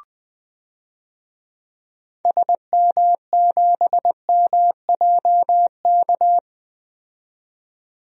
Morse code, heard as a keyed tone, about 20 wpm, 700 Hz. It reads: SM7MJK
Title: DXCC-100 Call Signs - 6 Characters - 20wpm